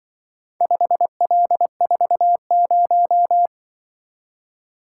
Morse code, heard as a keyed tone, about 23 wpm, 700 Hz. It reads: 5L40
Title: Speed-Racing - Four Letter-Number Combinations - 20wpm